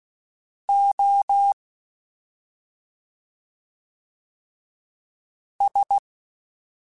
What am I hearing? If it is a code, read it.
Morse code, heard as a keyed tone, about 16 wpm, spaced as for 2 wpm, 800 Hz. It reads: OS
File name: Special INGs-mp3s / Special INGs-mp3s / Slow ing @10